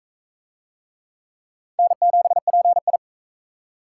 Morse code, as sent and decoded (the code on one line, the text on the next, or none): -. --... .--. ..
N7PI